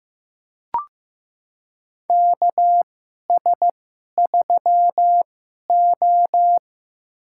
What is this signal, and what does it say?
Morse code, heard as a keyed tone, about 15 wpm, spaced as for 11 wpm, 700 Hz. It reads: KS3O
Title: USA Call Signs - 2x Character Spacing - 15wpm